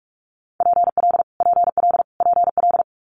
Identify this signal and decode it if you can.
Morse code, heard as a keyed tone, about 46 wpm, 700 Hz. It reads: PL PL PL
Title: Multi-letter Phonemes Repeated 3x 45wpm